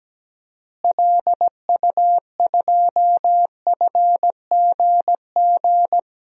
Morse code, heard as a keyed tone, about 17 wpm, 700 Hz. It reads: LU2FGG
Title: DXCC-100 Call Signs - 6 Characters - Repeated 2x - 17wpm